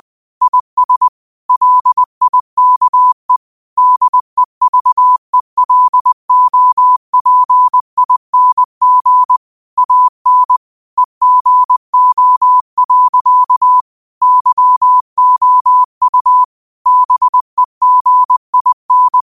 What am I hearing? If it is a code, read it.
Morse code, heard as a keyed tone, about 20 wpm, 1000 Hz. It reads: IS LIKE DEVELOPING AN EGO. YOU BEGIN
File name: qotdfiles / 20wpm